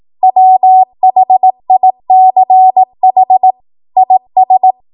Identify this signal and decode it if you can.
Morse code, heard as a keyed tone, about 18 wpm, 750 Hz. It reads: WHICH IS